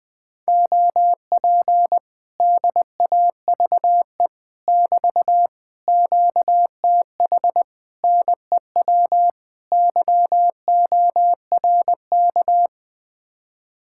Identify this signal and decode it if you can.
Morse code, heard as a keyed tone, about 21 wpm, 700 Hz. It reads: OP DAVE <BT> QTH NEW YORK